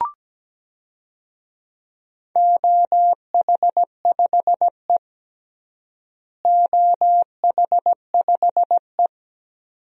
Morse code, heard as a keyed tone, about 17 wpm, 700 Hz. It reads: OH5E OH5E
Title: DXCC-100 Call Signs - 4 Characters - Repeated 2x - 17wpm